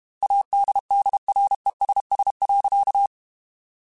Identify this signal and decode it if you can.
Morse code, heard as a keyed tone, about 32 wpm, 800 Hz. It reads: ADDRESS.
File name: Chuckles / Chuckles-mp3s / Temptation @30